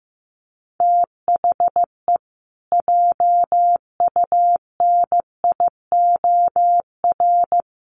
Morse code, heard as a keyed tone, about 15 wpm, 700 Hz. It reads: THE JUNIOR
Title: Sentences from Top 1500 Words 15wpm